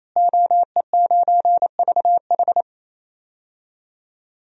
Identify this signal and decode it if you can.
Morse code, heard as a keyed tone, about 28 wpm, 700 Hz. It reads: OE9VH
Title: DXCC-100 Call Signs - 5 Characters - Repeated 3x - 28wpm